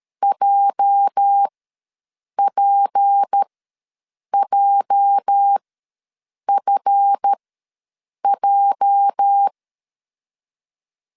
Morse code, heard as a keyed tone, about 13 wpm, spaced as for 7 wpm, 800 Hz. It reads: JPJFJ
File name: LEZIONI / 8 YJ45 / 8 YJ ws 12wpm 3342